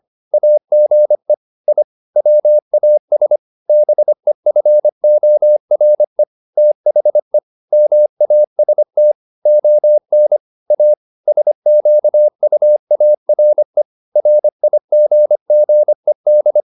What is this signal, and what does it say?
Morse code, heard as a keyed tone, about 25 wpm, 600 Hz. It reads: AGE I WAS BEFORE THE MAST ON A SQUARE RIGGED
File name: the old man and the sea part 1 25wpm